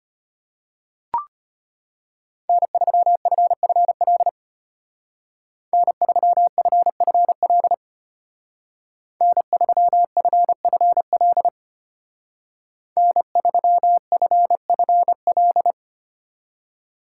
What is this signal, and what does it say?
Morse code, heard as a keyed tone, about 33 wpm, 700 Hz. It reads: N3FFL N3FFL N3FFL N3FFL